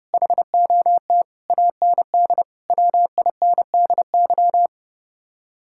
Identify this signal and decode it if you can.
Morse code, heard as a keyed tone, about 36 wpm, 700 Hz. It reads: HOT AND WINDY
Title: Speed-Racing Common QSO Phrases - 30wpm